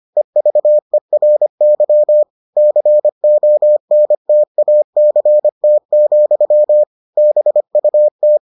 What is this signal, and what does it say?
Morse code, heard as a keyed tone, about 25 wpm, 600 Hz. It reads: EVERY CONTACT, BUT